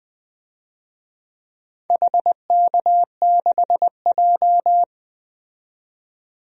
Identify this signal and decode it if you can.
Morse code, heard as a keyed tone, about 20 wpm, 700 Hz. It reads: HK6J
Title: DXCC-100 Call Signs - 4 Characters - 20wpm